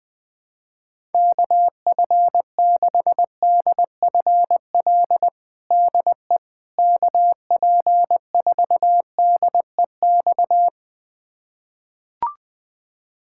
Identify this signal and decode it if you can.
Morse code, heard as a keyed tone, about 21 wpm, 700 Hz. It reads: KF6DFL DE KP4DEX
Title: USA Callsign DE Callsign - 6 Characters - 20wpm